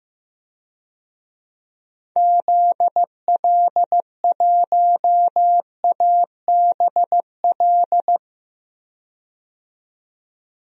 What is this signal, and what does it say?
Morse code, heard as a keyed tone, about 15 wpm, 700 Hz. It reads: ZL1ABL
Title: DXCC-100 Call Signs - 6 Characters - 15wpm